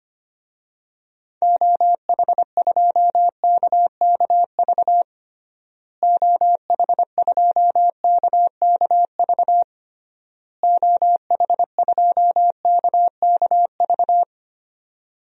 Morse code, heard as a keyed tone, about 25 wpm, 700 Hz. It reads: OH2KKV OH2KKV OH2KKV
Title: DXCC-100 Call Signs - 6 Characters - Repeated 3x - 25wpm